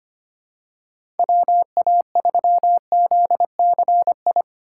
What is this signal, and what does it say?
Morse code, heard as a keyed tone, about 25 wpm, 700 Hz. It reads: WA3ZCI
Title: USA Call Signs - 6 Characters - 3x Character Spacing - 25wpm